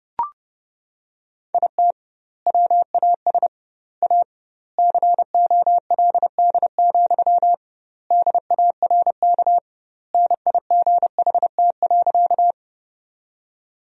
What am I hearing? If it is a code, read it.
Morse code, heard as a keyed tone, about 30 wpm, 700 Hz. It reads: IT WAS A COLD, DARK NIGHT.
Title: 2x Word-Spacing - Sentences from Top 500 Words - 30wpm